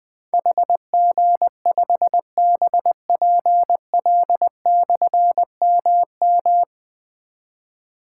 Morse code, heard as a keyed tone, about 20 wpm, 700 Hz. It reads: HG5BPL/MM
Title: Difficult Call Signs 20wpm